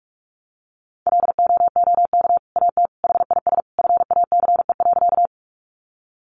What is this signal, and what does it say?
Morse code, heard as a keyed tone, about 44 wpm, 700 Hz. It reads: LOOK AT HIS FACE.
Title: Sentences from Top 300 Words 45wpm